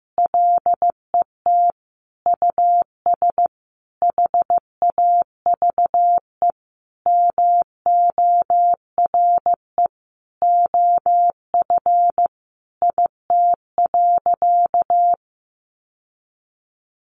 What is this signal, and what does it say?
Morse code, heard as a keyed tone, about 15 wpm, 700 Hz. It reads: LET US HAVE MORE OF IT.
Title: Sentences from Top 1300 Words 15wpm